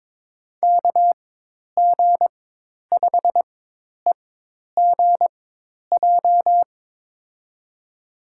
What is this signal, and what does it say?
Morse code, heard as a keyed tone, about 22 wpm, spaced as for 10 wpm, 700 Hz. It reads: KG5EGJ
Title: USA Call Signs - 4x Character Spacing - 22wpm